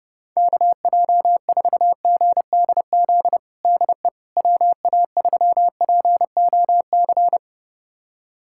Morse code, heard as a keyed tone, about 30 wpm, 700 Hz. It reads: KJ4GDZ DE WA3POC
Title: USA Callsign DE Callsign 30wpm